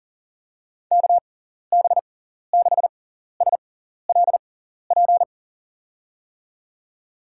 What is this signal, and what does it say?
Morse code, heard as a keyed tone, about 40 wpm, spaced as for 14 wpm, 700 Hz. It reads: KB6SLP
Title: USA Call Signs - 6x Character Spacing - 40wpm